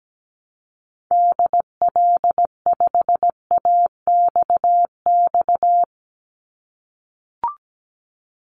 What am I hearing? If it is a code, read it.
Morse code, heard as a keyed tone, about 17 wpm, 700 Hz. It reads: DL5AXX
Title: Call Signs - 17wpm